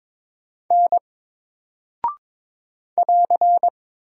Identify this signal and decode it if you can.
Morse code, heard as a keyed tone, about 22 wpm, 700 Hz. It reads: N <AR>
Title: Single Character (Letter:Number:Punctuation:Prosign) Alt 22wpm